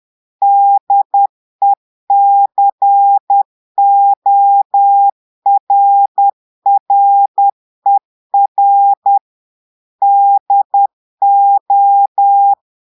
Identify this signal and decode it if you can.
Morse code, heard as a keyed tone, about 10 wpm, 800 Hz. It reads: DECORRER DO
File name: portugais texte en clair 2 vitesse 10 mots minutes